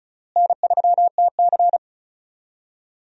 Morse code, heard as a keyed tone, about 34 wpm, 700 Hz. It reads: N3TC
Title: USA Call Signs - 4 Characters - Repeated 3x - 35wpm